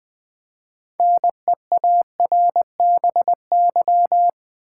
Morse code, heard as a keyed tone, about 20 wpm, 700 Hz. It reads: NEARBY
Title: Build A Word - 3 Letters At A Time - 6-Letter Words - 20wpm